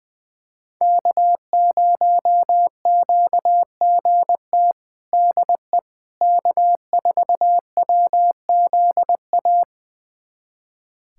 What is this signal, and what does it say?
Morse code, heard as a keyed tone, about 21 wpm, 700 Hz. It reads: K0QGT DE K4WZA
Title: USA Callsign DE Callsign - 5 Characters - 20wpm